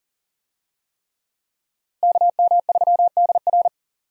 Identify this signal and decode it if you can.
Morse code, heard as a keyed tone, about 40 wpm, 700 Hz. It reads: KM3DR
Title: USA Call Signs - 5 Characters - Repeated 3x - 40wpm